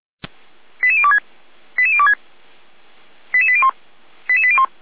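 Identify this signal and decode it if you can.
Five-tone selective call (ZVEI), heard as quick runs of five tones, tones of about 70 ms. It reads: 88026 88026 88801 88801
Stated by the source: ZVEI five-tone call received by SDR, from the fire brigade Feuerwehr Orlamünde server.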